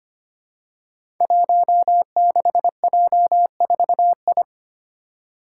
Morse code, heard as a keyed tone, about 25 wpm, 700 Hz. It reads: 16J4I
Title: Five Challenging Letter-Number Combinations 25wpm